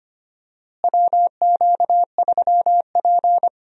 Morse code, heard as a keyed tone, about 25 wpm, 700 Hz. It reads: WQ3P
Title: USA Call Signs - 4 Characters - Repeated 3x - 25wpm